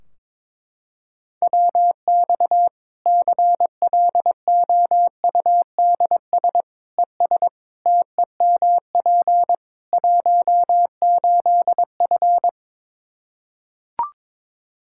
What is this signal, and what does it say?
Morse code, heard as a keyed tone, about 21 wpm, 700 Hz. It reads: WX CLOUDS ES TEMP 18F
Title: Practice QSO - Exchange 2 - Info - 22wpm